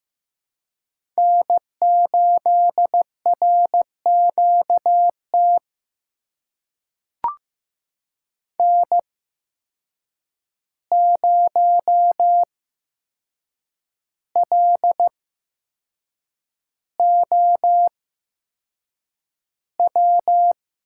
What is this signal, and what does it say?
Morse code, heard as a keyed tone, about 15 wpm, spaced as for 6 wpm, 700 Hz. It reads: N8RQT N0LOW
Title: USA Call Signs - 5 Characters - 8x Character Spacing - 15wpm